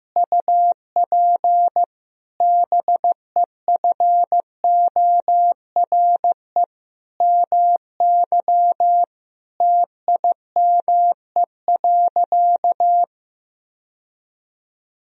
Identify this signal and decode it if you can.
Morse code, heard as a keyed tone, about 15 wpm, 700 Hz. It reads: UP BEFORE MY TIME.
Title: Sentences from Top 1000 Words - Review Part 1 - 15wpm